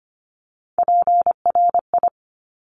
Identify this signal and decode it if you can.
Morse code, heard as a keyed tone, about 25 wpm, 700 Hz. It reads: PRI
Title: Top Three Letter Combinations in Words 25wpm